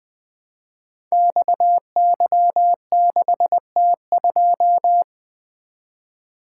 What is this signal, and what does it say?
Morse code, heard as a keyed tone, about 20 wpm, 700 Hz. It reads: XY6T2
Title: Code Groups - 5 Letter:Numbers - 20wpm